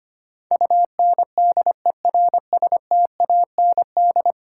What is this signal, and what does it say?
Morse code, heard as a keyed tone, about 25 wpm, 700 Hz. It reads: UNDERSTAND
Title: Sentences from Top 600 Words - New Word Drill - 25wpm